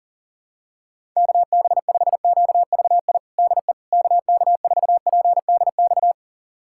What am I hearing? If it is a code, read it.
Morse code, heard as a keyed tone, about 40 wpm, 700 Hz. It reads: KB5QVI DE KK4PDX
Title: USA Callsign DE Callsign - 6 Characters - 40wpm